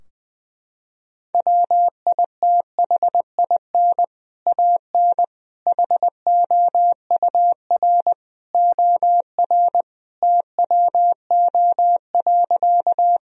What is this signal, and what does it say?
Morse code, heard as a keyed tone, about 20 wpm, 700 Hz. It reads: WITHIN AN HOUR OR TWO.
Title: Sentences from Top 1000 Words - Review Part 4 - 20wpm